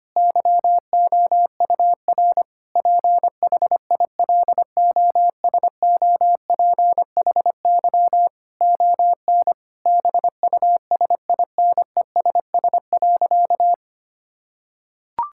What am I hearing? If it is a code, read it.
Morse code, heard as a keyed tone, about 25 wpm, 700 Hz. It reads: YOUR PHILOSOPHY ON BUSINESS.
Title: Sentences from Top 1000 Words - Review Part 6 - 25wpm